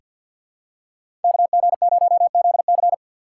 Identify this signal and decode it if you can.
Morse code, heard as a keyed tone, about 48 wpm, 700 Hz. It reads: KG0ZC
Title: USA Call Signs - 4x Character Spacing - 50wpm